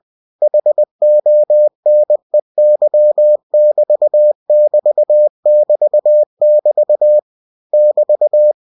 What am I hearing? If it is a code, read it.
Morse code, heard as a keyed tone, about 20 wpm, 600 Hz. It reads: HONEY<BT><BT><BT><BT> <BT>
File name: cw-books / winnie-the-pooh / winnie-20wpm-page-0030